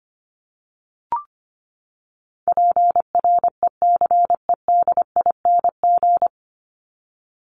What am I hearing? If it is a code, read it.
Morse code, heard as a keyed tone, about 25 wpm, 700 Hz. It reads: PRECEDING
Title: ING Suffix - Top 100 Words with Base Words 25wpm